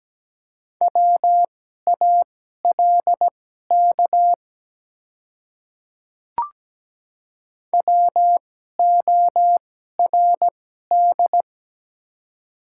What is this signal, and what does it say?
Morse code, heard as a keyed tone, about 17 wpm, spaced as for 12 wpm, 700 Hz. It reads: WALK WORD
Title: 2x Character Spacing - Top 300 Words - 17wpm